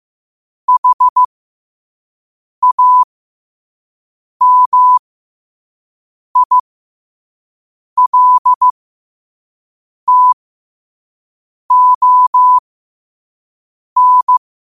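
Morse code, heard as a keyed tone, about 15 wpm, spaced as for 5 wpm, 1000 Hz. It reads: HAMILTON
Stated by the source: Morse code practice at 5 wpm.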